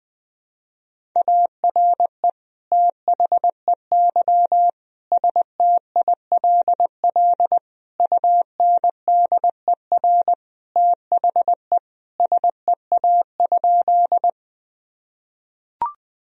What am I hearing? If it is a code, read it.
Morse code, heard as a keyed tone, about 21 wpm, 700 Hz. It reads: ARE THEY STILL UNDER THE SEA?
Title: Sentences from Top 1000 Words - Review Part 1 - 20wpm